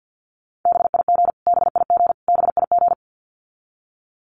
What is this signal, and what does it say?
Morse code, heard as a keyed tone, about 48 wpm, 700 Hz. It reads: BIG BIG BIG
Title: Top 200 Words Repeated 50wpm